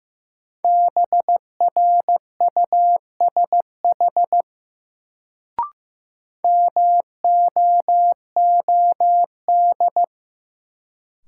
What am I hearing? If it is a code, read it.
Morse code, heard as a keyed tone, about 15 wpm, 700 Hz. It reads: BRUSH MOOD
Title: Sentences from Top 1100 Words - New Word Drill Alt - 15wpm